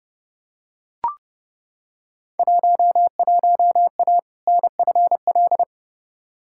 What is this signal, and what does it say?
Morse code, heard as a keyed tone, about 31 wpm, 700 Hz. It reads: 11A NFL
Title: Field Day Exchanges Encore - 30wpm